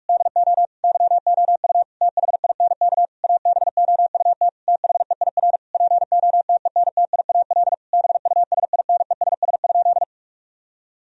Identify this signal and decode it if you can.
Morse code, heard as a keyed tone, about 41 wpm, 700 Hz. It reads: DO YOU THINK ABOUT THEIR POTENTIAL BUSINESS?
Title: Sentences from Top 1000 Words - Review Part 4 - 45wpm